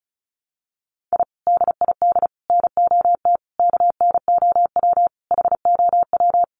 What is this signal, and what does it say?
Morse code, heard as a keyed tone, about 34 wpm, 700 Hz. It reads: I DID NOT KNOW HOW
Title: Sets of 5 Words Encore 35wpm